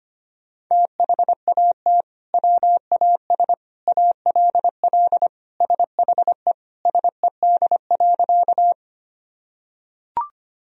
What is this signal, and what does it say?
Morse code, heard as a keyed tone, about 25 wpm, 700 Hz. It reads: THAT WAS ALL SHE SED.
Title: Sentences from Top 100 Words and Abbreviations 25wpm